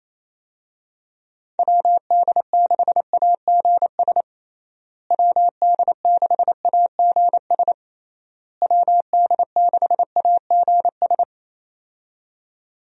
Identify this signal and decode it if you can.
Morse code, heard as a keyed tone, about 28 wpm, 700 Hz. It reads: WD6AGS WD6AGS WD6AGS
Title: USA Call Signs - 6 Characters - Repeated 3x - 28wpm